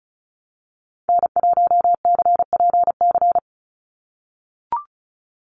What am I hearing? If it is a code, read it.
Morse code, heard as a keyed tone, about 34 wpm, 700 Hz. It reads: N1CPC
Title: USA Call Signs 35wpm